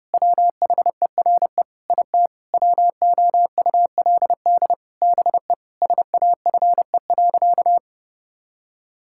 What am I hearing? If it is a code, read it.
Morse code, heard as a keyed tone, about 30 wpm, 700 Hz. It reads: WHERE IT WOULD BE SAFE.
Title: Sentences from Top 1600 Words 30wpm